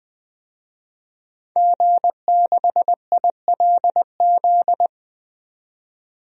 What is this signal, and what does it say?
Morse code, heard as a keyed tone, about 20 wpm, 700 Hz. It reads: G6ILZ
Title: DXCC-100 Call Signs - 5 Characters - 20wpm